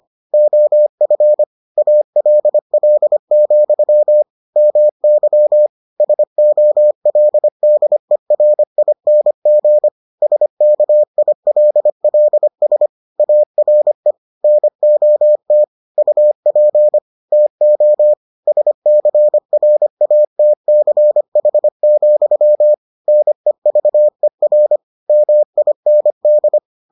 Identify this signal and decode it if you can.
Morse code, heard as a keyed tone, about 25 wpm, 600 Hz. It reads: OF ALL, MY SOLDERING SKILLS ARE NOT UP TO SCRATCH, NEVER MIND